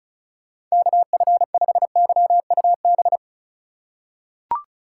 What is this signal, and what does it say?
Morse code, heard as a keyed tone, about 36 wpm, 700 Hz. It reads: KF5YUB
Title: K1USN SST QSO Practice - 35wpm